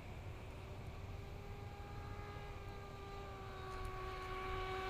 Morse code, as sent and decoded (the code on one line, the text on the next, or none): none